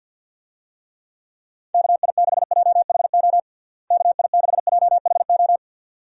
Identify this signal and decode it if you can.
Morse code, heard as a keyed tone, about 48 wpm, 700 Hz. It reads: KI6JHO KI6JHO
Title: USA Call Signs - 6 Characters - Repeated 2x - 50wpm